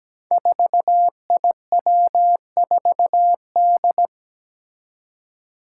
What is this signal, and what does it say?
Morse code, heard as a keyed tone, about 17 wpm, 700 Hz. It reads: 4IW4D
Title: Five Challenging Letter-Number Combinations 17wpm